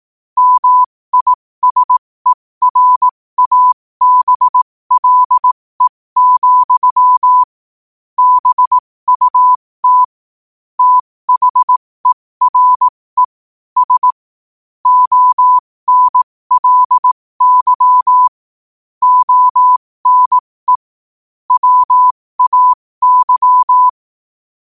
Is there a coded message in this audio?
Yes: Morse code from 0.4 to 23.9 s, heard as a keyed tone, about 18 wpm, spaced as for 15 wpm, 1000 Hz. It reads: MISERABLE, BUT THERE S ONLY ONE WAY